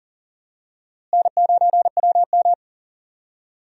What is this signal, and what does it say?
Morse code, heard as a keyed tone, about 38 wpm, 700 Hz. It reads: N9WM
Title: USA Call Signs - 4 Characters - Repeated 2x - 40wpm